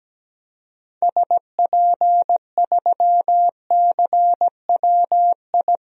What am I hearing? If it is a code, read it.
Morse code, heard as a keyed tone, about 17 wpm, 700 Hz. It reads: SP3CWI
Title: DXCC-100 Call Signs - 6 Characters - 17wpm